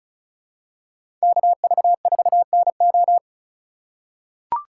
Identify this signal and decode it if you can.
Morse code, heard as a keyed tone, about 34 wpm, 700 Hz. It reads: KV4NO
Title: USA Call Signs - 5 Characters - 6x Character Spacing - 35wpm